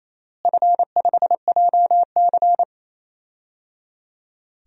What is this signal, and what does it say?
Morse code, heard as a keyed tone, about 28 wpm, 700 Hz. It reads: F5JC
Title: DXCC-100 Call Signs - 4 Characters - Repeated 3x - 28wpm